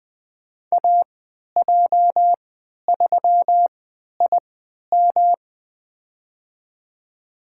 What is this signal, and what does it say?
Morse code, heard as a keyed tone, about 21 wpm, spaced as for 11 wpm, 700 Hz. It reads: AJ3IM
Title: USA Call Signs - 5 Characters - 3x Character Spacing - 20wpm